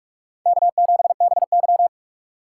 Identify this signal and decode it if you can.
Morse code, heard as a keyed tone, about 43 wpm, 700 Hz. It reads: K7BY